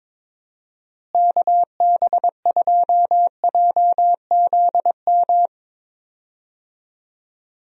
Morse code, heard as a keyed tone, about 22 wpm, 700 Hz. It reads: KB2JZM
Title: USA Call Signs - 6 Characters - Repeated 3x - 22wpm